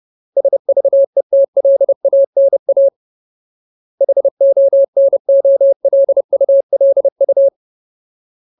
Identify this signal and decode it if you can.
Morse code, heard as a keyed tone, about 30 wpm, 550 Hz. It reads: SVETLANA HONOLULU
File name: Ultimate Word List / 30WPM Ultimate